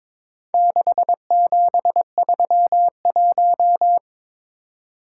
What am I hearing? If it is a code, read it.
Morse code, heard as a keyed tone, about 21 wpm, 700 Hz. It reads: 6731
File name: practice / 22wpm / Four Letter-Number Combinations - 22wpm